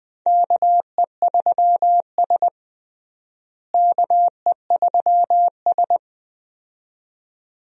Morse code, heard as a keyed tone, about 20 wpm, 700 Hz. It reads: KE3S KE3S
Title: USA Call Signs - 4 Characters - Repeated 2x - 20wpm